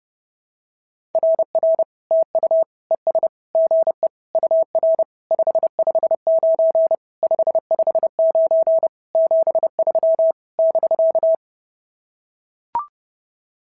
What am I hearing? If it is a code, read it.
Morse code, heard as a keyed tone, about 30 wpm, 690 Hz. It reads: RR TU ES GE UR 559 559 73 <BK>
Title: SOTA QSO Practice 30wpm